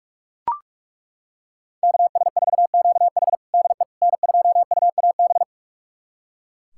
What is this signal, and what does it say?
Morse code, heard as a keyed tone, about 42 wpm, 700 Hz. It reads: KS4QH DE N2UAB
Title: USA Callsign DE Callsign - 5 Characters - 45wpm